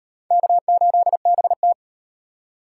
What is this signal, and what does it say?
Morse code, heard as a keyed tone, about 37 wpm, 700 Hz. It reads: K8BT